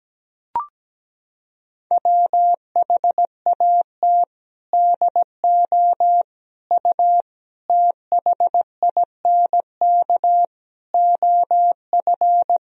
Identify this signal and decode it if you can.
Morse code, heard as a keyed tone, about 17 wpm, 700 Hz. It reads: WHAT DO U THINK OF
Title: Sentences from Top 200 Words and Abbreviations 17wpm